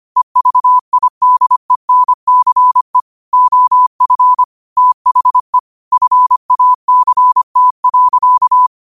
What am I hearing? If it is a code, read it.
Morse code, heard as a keyed tone, about 25 wpm, 1000 Hz. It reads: EVIDENCE OF THE FACT.
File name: qotdfiles / 25wpm